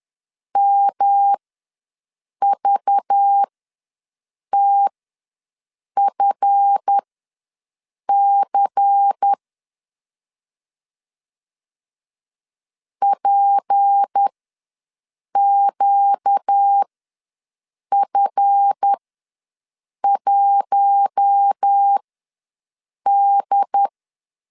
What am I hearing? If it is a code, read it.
Morse code, heard as a keyed tone, about 10 wpm, spaced as for 6 wpm, 800 Hz. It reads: MVTFC PQF1D
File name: LEZIONI / 7 GF9Z / 7 GF ws 10wpm 3918